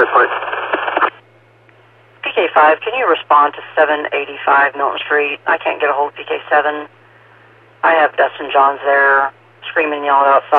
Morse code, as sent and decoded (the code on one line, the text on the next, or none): none